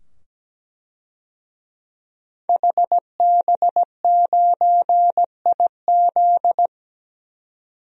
Morse code, heard as a keyed tone, about 17 wpm, 700 Hz. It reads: HB9IZ
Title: DXCC-100 Call Signs - 5 Characters - 17wpm